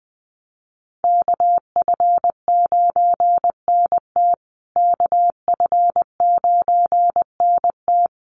Morse code, heard as a keyed tone, about 20 wpm, 700 Hz. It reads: KF9NT KF9NT
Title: USA Call Signs Repeated 2x 20wpm